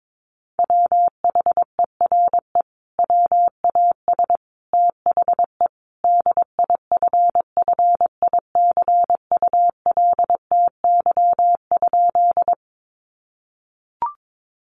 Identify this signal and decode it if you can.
Morse code, heard as a keyed tone, about 22 wpm, 700 Hz. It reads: WHERE WAS THE DIFFICULTY?
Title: Sentences from Top 1500 Words 22wpm